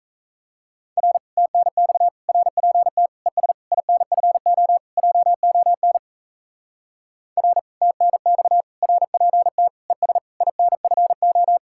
Morse code, heard as a keyed tone, about 34 wpm, 700 Hz. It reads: R TNX RPT ES INFO JON R TNX RPT ES INFO
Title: Speed-Racing Common QSO Phrases - 28wpm